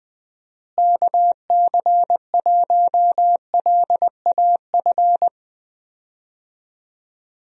Morse code, heard as a keyed tone, about 20 wpm, 700 Hz. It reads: KC1LAF
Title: USA Call Signs - 6 Characters - Repeated 3x - 20wpm